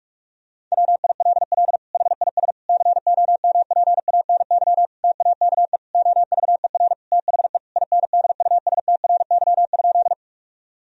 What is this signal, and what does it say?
Morse code, heard as a keyed tone, about 41 wpm, 700 Hz. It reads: WILL HIS COMPANY TAKE OVER THE INDUSTRY?
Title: Sentences from Top 1000 Words - Review Part 4 - 45wpm